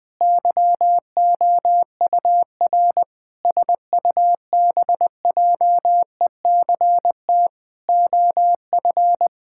Morse code, heard as a keyed tone, about 21 wpm, 700 Hz. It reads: YOUR SUBJECT OF
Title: Sentences from Top 1000 Words - Review Part 5 - 20wpm